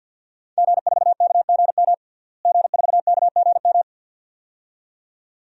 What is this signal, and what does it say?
Morse code, heard as a keyed tone, about 51 wpm, 700 Hz. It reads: G4KGM G4KGM
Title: DXCC-100 Call Signs - 5 Characters - Repeated 2x - 50wpm